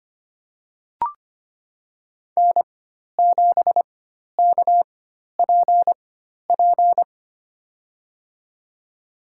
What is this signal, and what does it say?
Morse code, heard as a keyed tone, about 25 wpm, spaced as for 12 wpm, 700 Hz. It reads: N7KPP
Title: USA Call Signs - 4x Character Spacing - 25wpm